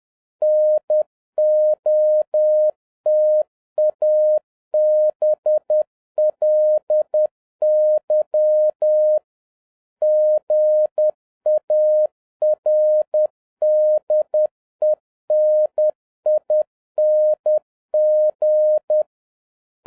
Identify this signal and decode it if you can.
Morse code, heard as a keyed tone, about 10 wpm, 610 Hz. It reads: NOTABLY GARDENING